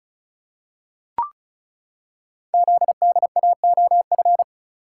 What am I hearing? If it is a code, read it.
Morse code, heard as a keyed tone, about 36 wpm, 700 Hz. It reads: ZDAOF
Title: Code Groups - 5 Letter:Numbers - 35wpm